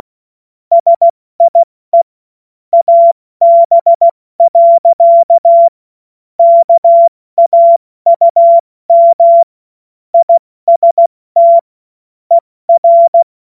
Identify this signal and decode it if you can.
Morse code, heard as a keyed tone, about 16 wpm, 680 Hz. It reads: SIE AB. KAUM IST ER
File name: Morsekurs / Witze / Witz 10.Oktober 2023 70 F80